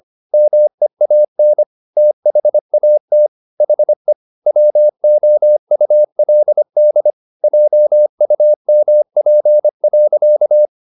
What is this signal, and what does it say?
Morse code, heard as a keyed tone, about 25 wpm, 600 Hz. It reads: MEAN THAT HE WOULD JUMP.
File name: the old man and the sea part 1 25wpm